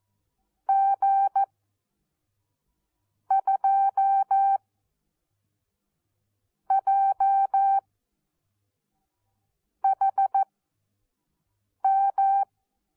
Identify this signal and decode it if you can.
Morse code, heard as a keyed tone, about 14 wpm, spaced as for 4 wpm, 780 Hz. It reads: G2JHM
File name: Onlinelehrgang / morsen / DJ4UF Morsen14